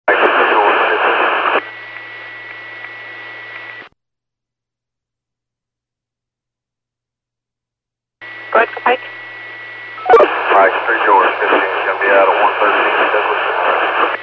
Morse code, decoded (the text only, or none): none